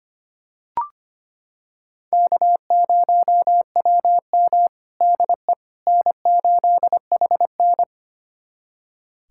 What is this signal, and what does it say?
Morse code, heard as a keyed tone, about 25 wpm, 700 Hz. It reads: K0WM DE N8HN
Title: USA Callsign DE Callsign - 4 Characters - 25wpm